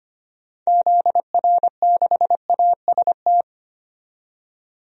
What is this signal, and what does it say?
Morse code, heard as a keyed tone, about 25 wpm, 700 Hz. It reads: ZR6AST
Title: DXCC-100 Call Signs - 6 Characters - 25wpm